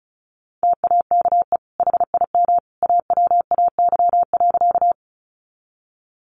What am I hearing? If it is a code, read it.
Morse code, heard as a keyed tone, about 34 wpm, 700 Hz. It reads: TAKE HIM AWAY.
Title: Sentences from Top 600 Words 35wpm